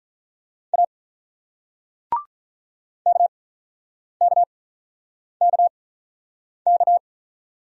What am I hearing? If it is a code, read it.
Morse code, heard as a keyed tone, about 47 wpm, spaced as for 9 wpm, 700 Hz. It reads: A KKKK